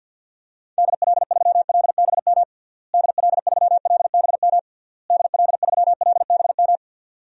Morse code, heard as a keyed tone, about 49 wpm, 700 Hz. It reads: DL3LBM DL3LBM DL3LBM
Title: DXCC-100 Call Signs - 6 Characters - Repeated 3x - 50wpm